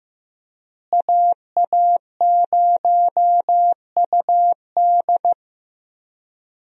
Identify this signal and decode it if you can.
Morse code, heard as a keyed tone, about 15 wpm, 700 Hz. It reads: AA0UD
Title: USA Call Signs - 5 Characters - Repeated 3x - 15wpm